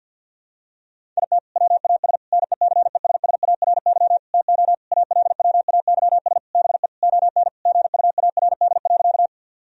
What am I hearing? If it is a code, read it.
Morse code, heard as a keyed tone, about 48 wpm, 700 Hz. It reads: IT WAS NECESSARY TO ALWAYS BE ON GUARD.
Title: Sentences from Top 1000 Words - Review Part 4 - 50wpm